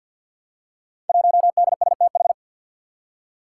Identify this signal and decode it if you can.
Morse code, heard as a keyed tone, about 50 wpm, 700 Hz. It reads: 1DSTH
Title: Five Challenging Letter-Number Combinations 50wpm